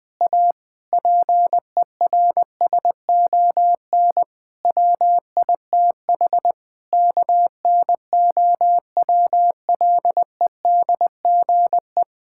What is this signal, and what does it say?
Morse code, heard as a keyed tone, about 20 wpm, 700 Hz. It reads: A PERSON WITH KNOWLEDGE
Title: Sets of 8 Words - 20wpm